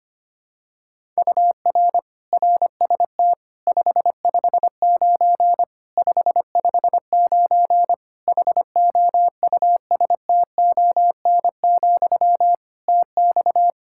Morse code, heard as a keyed tone, about 25 wpm, 700 Hz. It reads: UR RST 559 559 HOUSTON, TX
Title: K1USN SST QSO Practice 25wpm